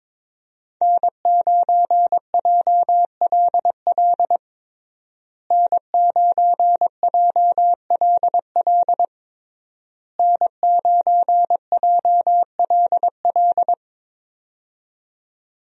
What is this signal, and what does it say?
Morse code, heard as a keyed tone, about 22 wpm, 700 Hz. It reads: N9JLL N9JLL N9JLL
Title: USA Call Signs - 5 Characters - Repeated 3x - 22wpm